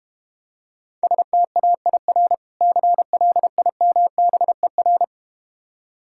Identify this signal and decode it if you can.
Morse code, heard as a keyed tone, about 32 wpm, 700 Hz. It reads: STAIR CLIMBER